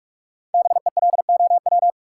Morse code, heard as a keyed tone, about 44 wpm, 700 Hz. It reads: BELOW
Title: Sentences from Top 1200 Words - New Word Drill Alt - 45wpm